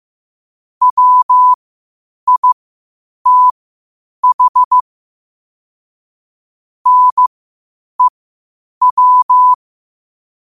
Morse code, heard as a keyed tone, about 15 wpm, spaced as for 8 wpm, 1000 Hz. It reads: WITH NEW
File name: qotdfiles / 8wpm